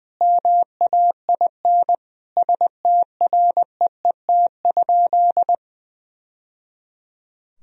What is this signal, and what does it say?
Morse code, heard as a keyed tone, about 20 wpm, 700 Hz. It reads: MAIN STREET?
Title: Sentences from Top 1000 Words - Review Part 2 - 20wpm